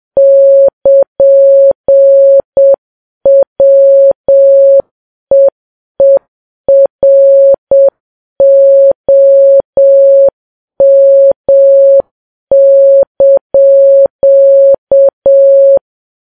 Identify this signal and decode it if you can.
Morse code, heard as a keyed tone, about 7 wpm, 550 Hz. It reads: <KN>WEEROM)
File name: Files Web tls / Morse on9fv / morse7kps